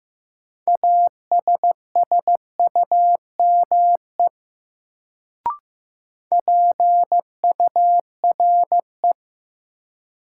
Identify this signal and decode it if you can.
Morse code, heard as a keyed tone, about 15 wpm, 700 Hz. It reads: ASSUME PURE
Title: Sentences from Top 1500 Words - New Word Drill - 15wpm